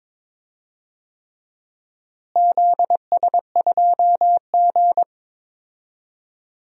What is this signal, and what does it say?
Morse code, heard as a keyed tone, about 22 wpm, 700 Hz. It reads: ZS2G
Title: DXCC-100 Call Signs - 4 Characters - Repeated 3x - 22wpm